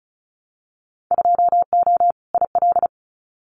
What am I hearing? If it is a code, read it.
Morse code, heard as a keyed tone, about 34 wpm, 700 Hz. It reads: 2O IL
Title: Winter Field Day Exchanges 35wpm